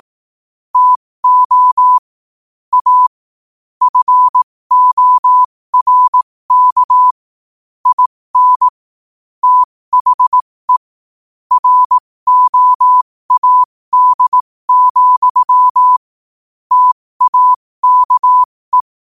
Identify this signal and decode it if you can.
Morse code, heard as a keyed tone, about 18 wpm, spaced as for 15 wpm, 1000 Hz. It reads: TO A FORK IN THE ROAD, TAKE
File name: qotdfiles / 15wpm